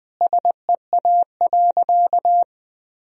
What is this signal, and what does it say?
Morse code, heard as a keyed tone, about 20 wpm, 700 Hz. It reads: SEA.
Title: Sentences from Top 1000 Words - Review Part 2 - 20wpm